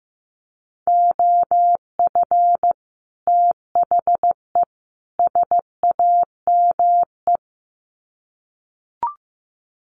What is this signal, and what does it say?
Morse code, heard as a keyed tone, about 15 wpm, 700 Hz. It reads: OF THE SAME
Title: Sets of 3 Words 15wpm